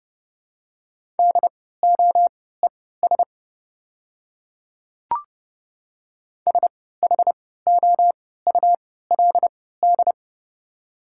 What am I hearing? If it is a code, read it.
Morse code, heard as a keyed tone, about 30 wpm, spaced as for 17 wpm, 700 Hz. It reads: DOES SHOULD
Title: 3x Character Spacing - Top 300 Words - 30wpm